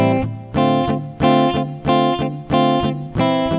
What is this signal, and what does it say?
There is no code.